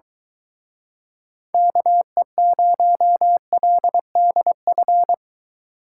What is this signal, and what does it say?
Morse code, heard as a keyed tone, about 23 wpm, 700 Hz. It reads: KE0LDF